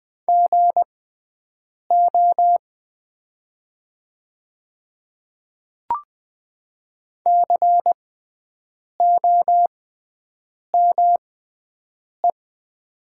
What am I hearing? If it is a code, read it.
Morse code, heard as a keyed tone, about 20 wpm, spaced as for 7 wpm, 700 Hz. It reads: GO COME
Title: 6x Character Spacing - Top 100 Words - 20wpm